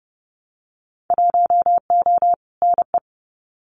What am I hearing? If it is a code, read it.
Morse code, heard as a keyed tone, about 30 wpm, 700 Hz. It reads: 1O NE
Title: Winter Field Day Exchanges 30wpm